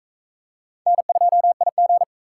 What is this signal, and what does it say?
Morse code, heard as a keyed tone, about 36 wpm, 700 Hz. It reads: N2IG